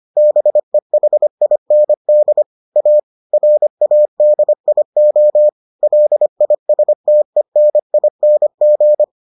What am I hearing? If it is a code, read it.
Morse code, heard as a keyed tone, about 25 wpm, 600 Hz. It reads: BEHIND A RADIO LISTENING